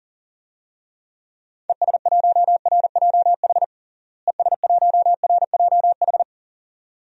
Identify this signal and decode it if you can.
Morse code, heard as a keyed tone, about 37 wpm, 700 Hz. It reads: ES1RJH ES1RJH
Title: DXCC-100 Call Signs - 6 Characters - Repeated 2x - 40wpm